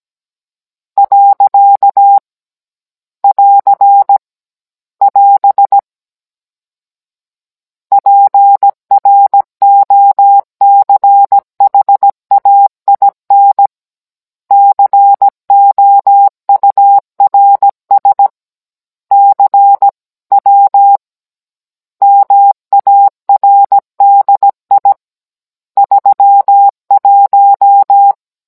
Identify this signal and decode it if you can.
Morse code, heard as a keyed tone, about 17 wpm, 800 Hz. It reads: . <AR> <AS> PROCHAIN COURS CW MARDI 31
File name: Cours-NR19-16mm